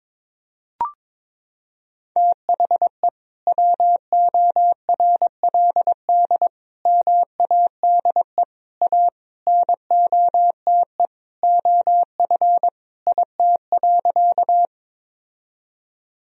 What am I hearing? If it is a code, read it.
Morse code, heard as a keyed tone, about 22 wpm, 700 Hz. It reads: THE WORLD MADE A NOTE OF IT.
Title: Sentences from Top 1000 Words - Review Part 2 - 22wpm